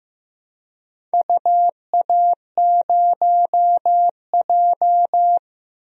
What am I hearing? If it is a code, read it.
Morse code, heard as a keyed tone, about 15 wpm, 700 Hz. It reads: UA0J
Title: DXCC-100 Call Signs - 4 Characters - Repeated 3x - 15wpm